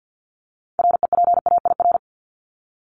Morse code, heard as a keyed tone, about 50 wpm, 700 Hz. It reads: REPAIR
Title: Top 1501-1750 Words 50wpm